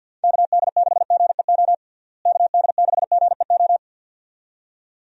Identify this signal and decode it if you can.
Morse code, heard as a keyed tone, about 53 wpm, 700 Hz. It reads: KD6GEO KD6GEO
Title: USA Call Signs - 6 Characters - Repeated 2x - 50wpm